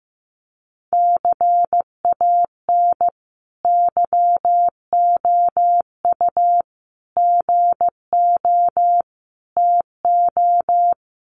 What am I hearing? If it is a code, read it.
Morse code, heard as a keyed tone, about 15 wpm, 700 Hz. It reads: CAN YOU GO TO
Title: Sentences from Top 1500 Words 15wpm